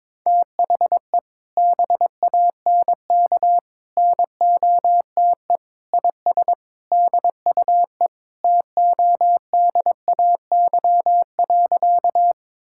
Morse code, heard as a keyed tone, about 22 wpm, 700 Hz. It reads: THE BANK NOTE IS DUE TODAY.
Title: Sentences from Top 1000 Words - Review Part 6 - 22wpm